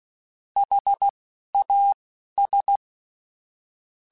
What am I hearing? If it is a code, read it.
Morse code, heard as a keyed tone, about 16 wpm, spaced as for 11 wpm, 800 Hz. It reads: HAS
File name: Technical Subjects / Technical Subjects-mp3s / AND Gate @10